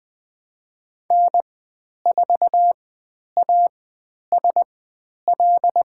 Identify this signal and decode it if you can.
Morse code, heard as a keyed tone, about 20 wpm, spaced as for 10 wpm, 700 Hz. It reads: N4ASL
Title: 095 Lesson 16g - Letter c - Call Signs 20wpm